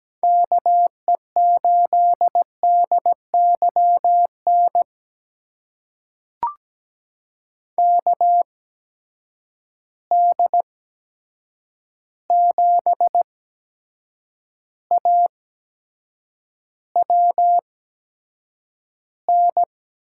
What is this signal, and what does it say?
Morse code, heard as a keyed tone, about 17 wpm, spaced as for 7 wpm, 700 Hz. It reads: KE8DYN KD7AWN
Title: USA Call Signs - 6 Characters - 8x Character Spacing - 17wpm